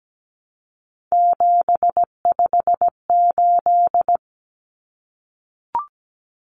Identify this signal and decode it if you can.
Morse code, heard as a keyed tone, about 17 wpm, 700 Hz. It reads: 758
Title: Numbers - 3-Digit 17wpm